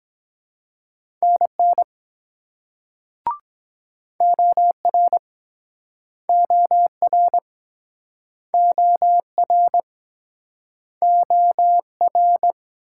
Morse code, heard as a keyed tone, about 23 wpm, 700 Hz. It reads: NN OR OR OR OR